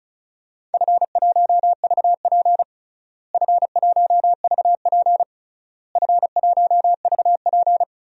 Morse code, heard as a keyed tone, about 34 wpm, 700 Hz. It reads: F1VP F1VP F1VP
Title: DXCC-100 Call Signs - 4 Characters - Repeated 3x - 35wpm